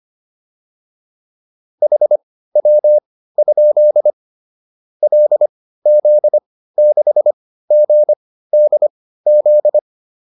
Morse code, heard as a keyed tone, about 25 wpm, spaced as for 15 wpm, 600 Hz. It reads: HW? LZ6GDZ
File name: QSO407 15